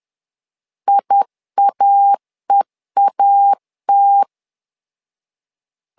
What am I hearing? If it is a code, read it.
Morse code, heard as a keyed tone, about 10 wpm, 800 Hz. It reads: IAEAT
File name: LEZIONI / 2 UHDN / 2 UHD 10wpm 1826